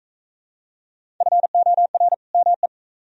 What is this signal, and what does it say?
Morse code, heard as a keyed tone, about 38 wpm, 700 Hz. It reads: FOR ME